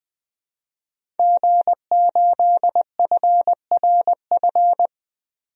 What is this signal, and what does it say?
Morse code, heard as a keyed tone, about 20 wpm, 700 Hz. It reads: G8FRF
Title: DXCC-100 Call Signs - 5 Characters - 20wpm